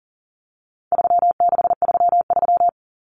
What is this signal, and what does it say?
Morse code, heard as a keyed tone, about 40 wpm, 700 Hz. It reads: 3633
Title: Common Cut Numbers - 4-Digit 40wpm